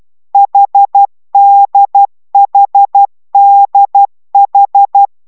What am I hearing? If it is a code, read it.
Morse code, heard as a keyed tone, about 12 wpm, 800 Hz. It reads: HDHDH